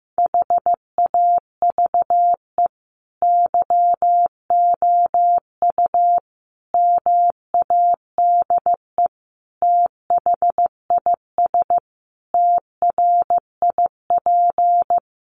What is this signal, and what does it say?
Morse code, heard as a keyed tone, about 15 wpm, 700 Hz. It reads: HAVE YOU MADE THIS TRIP
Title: Sentences from Top 1300 Words 15wpm